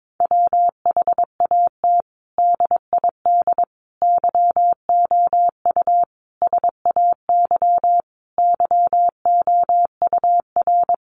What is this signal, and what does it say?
Morse code, heard as a keyed tone, about 22 wpm, 700 Hz. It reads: WHAT DID YOU SAY YOUR
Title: Sentences from Top 200 Words 22wpm